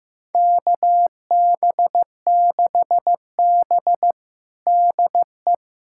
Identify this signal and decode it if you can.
Morse code, heard as a keyed tone, about 15 wpm, 700 Hz. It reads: KB6B DE
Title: USA Callsign DE Callsign - 4 Characters - 15wpm